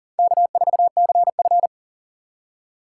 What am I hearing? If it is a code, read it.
Morse code, heard as a keyed tone, about 40 wpm, 700 Hz. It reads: K4CF
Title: USA Call Signs - 4 Characters - 6x Character Spacing - 40wpm